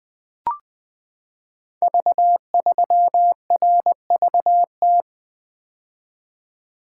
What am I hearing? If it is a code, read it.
Morse code, heard as a keyed tone, about 20 wpm, 700 Hz. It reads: V3RVT
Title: Code Groups - 5 Letter:Numbers - 20wpm